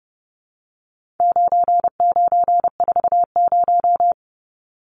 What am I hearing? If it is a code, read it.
Morse code, heard as a keyed tone, about 30 wpm, 700 Hz. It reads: 9940
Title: Numbers - 4-Digit 30wpm